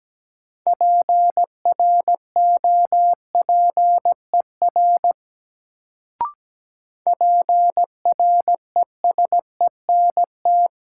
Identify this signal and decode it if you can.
Morse code, heard as a keyed tone, about 17 wpm, 700 Hz. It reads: PROPER PRESENT
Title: Sentences from Top 800 Words - New Word Drill - 17wpm